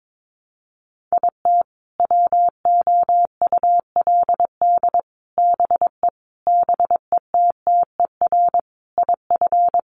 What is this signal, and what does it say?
Morse code, heard as a keyed tone, about 22 wpm, 700 Hz. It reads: IT WOULD BE BETTER IF
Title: Sentences from Top 400 Words 22wpm